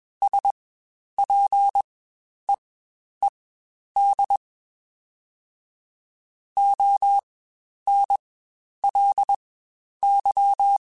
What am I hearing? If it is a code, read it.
Morse code, heard as a keyed tone, about 21 wpm, spaced as for 10 wpm, 800 Hz. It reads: SPEED ONLY
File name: Spaced Characters / Spaced Characters-mp3s / Weather5 1 sp @20